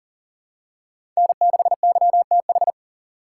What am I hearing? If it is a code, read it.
Morse code, heard as a keyed tone, about 39 wpm, 700 Hz. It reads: N6YTH
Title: USA Call Signs - 4x Character Spacing - 40wpm